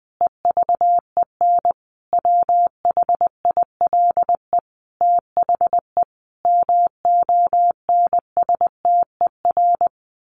Morse code, heard as a keyed tone, about 20 wpm, 700 Hz. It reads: EVEN WHILE THE MONSTER
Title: A Sound Of Thunder 20wpm